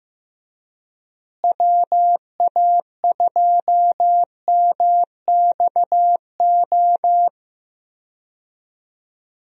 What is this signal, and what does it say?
Morse code, heard as a keyed tone, about 15 wpm, 700 Hz. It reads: WA2MXO
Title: USA Call Signs - 6 Characters - Repeated 2x - 15wpm